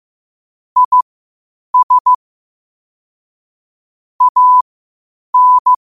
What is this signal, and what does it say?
Morse code, heard as a keyed tone, about 15 wpm, spaced as for 8 wpm, 1000 Hz. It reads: IS AN
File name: qotdfiles / 8wpm